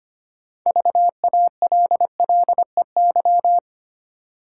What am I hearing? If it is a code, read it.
Morse code, heard as a keyed tone, about 25 wpm, 700 Hz. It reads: VALLEY
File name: practice / 25wpm / Easy Words with Hints - 25wpm